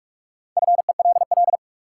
Morse code, heard as a keyed tone, about 47 wpm, 700 Hz. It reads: FELL
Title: Sets of 5 Words Encore - New Word Drill - 45wpm